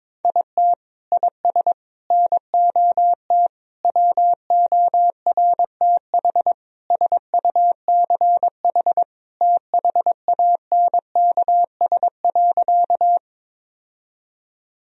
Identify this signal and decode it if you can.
Morse code, heard as a keyed tone, about 22 wpm, 700 Hz. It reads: IT IS NOT WORTH SUCH THANKS.
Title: Sentences from Top 1000 Words - Review Part 3 - 22wpm